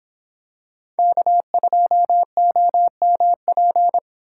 Morse code, heard as a keyed tone, about 24 wpm, 700 Hz. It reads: K2OMP